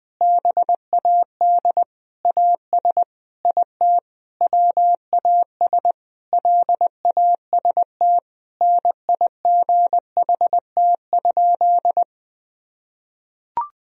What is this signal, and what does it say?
Morse code, heard as a keyed tone, about 20 wpm, 700 Hz. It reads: BAD AS IT WAS LAST NIGHT?
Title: Sentences from Top 1000 Words - Review Part 1 - 20wpm